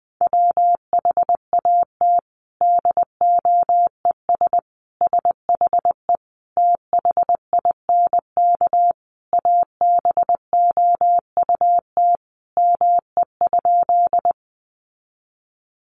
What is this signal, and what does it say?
Morse code, heard as a keyed tone, about 20 wpm, 700 Hz. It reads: WHAT DOES SHE THINK ABOUT ME?
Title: Sentences from Top 200 Words 20wpm